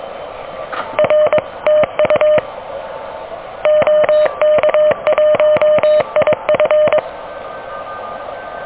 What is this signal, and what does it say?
Morse code, heard as a keyed tone, about 23 wpm, 610 Hz. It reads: R TU OK1IF
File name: OK QRP 2023 / OK QRP 2023